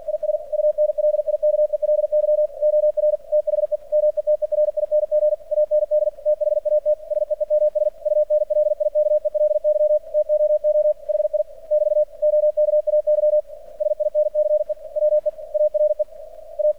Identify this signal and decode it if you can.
Morse code, heard as a keyed tone, about 50 wpm, 600 Hz. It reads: IS UNLIKELY OR THE METERING AND THAT SEEMS UNLIKELY TOO HI <BT> OKAY SINCE WE ARE A
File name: archived / CRSnet